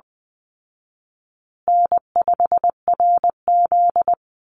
Morse code, heard as a keyed tone, about 20 wpm, 700 Hz. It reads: N5RZ
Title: Easy Call Signs Repeated 3x - 20wpm